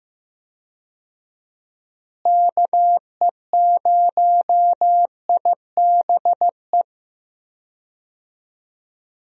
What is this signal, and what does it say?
Morse code, heard as a keyed tone, about 15 wpm, 700 Hz. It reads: KE0IBE
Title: USA Call Signs - 6 Characters - Repeated 3x - 15wpm